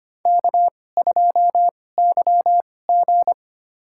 Morse code, heard as a keyed tone, about 25 wpm, spaced as for 18 wpm, 700 Hz. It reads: K2YG
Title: USA Call Signs - 2x Character Spacing - 25wpm